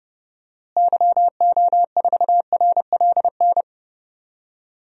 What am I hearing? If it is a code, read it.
Morse code, heard as a keyed tone, about 29 wpm, 700 Hz. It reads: YO4RLN
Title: DXCC-100 Call Signs - 6 Characters - Repeated 2x - 30wpm